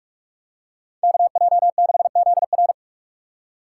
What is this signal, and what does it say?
Morse code, heard as a keyed tone, about 46 wpm, 700 Hz. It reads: KJ6ZR
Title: Speed Racing - USA Call Signs 45wpm